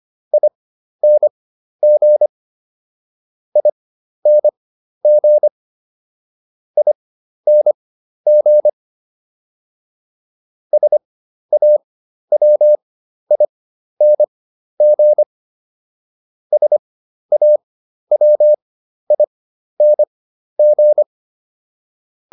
Morse code, heard as a keyed tone, about 25 wpm, spaced as for 12 wpm, 600 Hz. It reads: ING ING ING SAWING SAWING